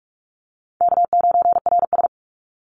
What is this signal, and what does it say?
Morse code, heard as a keyed tone, about 44 wpm, 700 Hz. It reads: K9RS
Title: Easy Call Signs Repeated 3x - 45wpm